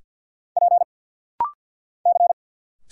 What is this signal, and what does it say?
Morse code, heard as a keyed tone, about 50 wpm, 700 Hz. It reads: P C
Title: Single Letters - Rapid-Fire 50wpm